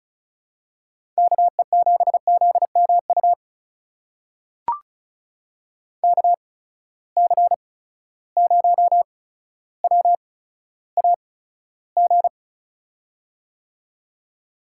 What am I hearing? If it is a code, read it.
Morse code, heard as a keyed tone, about 36 wpm, spaced as for 15 wpm, 700 Hz. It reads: KE7ZMU KC0WAG
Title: USA Call Signs - 8x Character Spacing - 35wpm